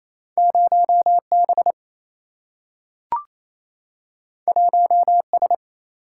Morse code, heard as a keyed tone, about 28 wpm, 700 Hz. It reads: 0B 1S
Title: Single-Letter Single-Number 28wpm